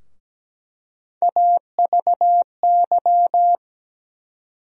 Morse code, heard as a keyed tone, about 17 wpm, 700 Hz. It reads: AVY
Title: USA Call Sign Postfixes - 3 Letters 17wpm